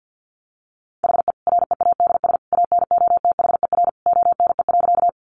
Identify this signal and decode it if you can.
Morse code, heard as a keyed tone, about 50 wpm, 700 Hz. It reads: HE READS ANOTHER ONE.
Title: Additional Easy Phrases 50wpm